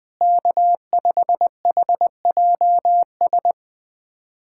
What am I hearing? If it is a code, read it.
Morse code, heard as a keyed tone, about 21 wpm, 700 Hz. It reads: K5HJS
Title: USA Callsign DE Callsign - 5 Characters - 20wpm